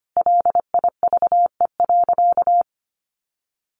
Morse code, heard as a keyed tone, about 25 wpm, 700 Hz. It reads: LIVE.
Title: Sentences from Top 300 Words 25wpm